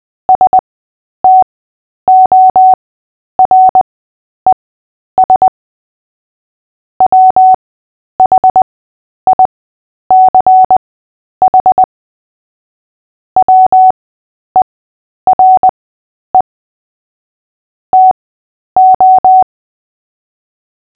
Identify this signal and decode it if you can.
Morse code, heard as a keyed tone, about 20 wpm, spaced as for 10 wpm, 750 Hz. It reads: STORES WHICH WERE TO
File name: Short-Story-7-10wpm